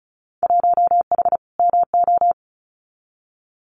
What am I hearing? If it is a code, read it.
Morse code, heard as a keyed tone, about 34 wpm, 700 Hz. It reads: 1H MO
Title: Winter Field Day Exchanges 35wpm